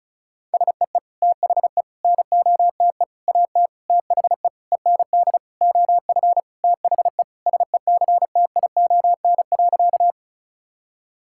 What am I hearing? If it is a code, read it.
Morse code, heard as a keyed tone, about 36 wpm, 690 Hz. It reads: SEE THE NOTE AT THE END OF THE SECTION.
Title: Sentences from Top 1000 Words - Review Part 5 - 35wpm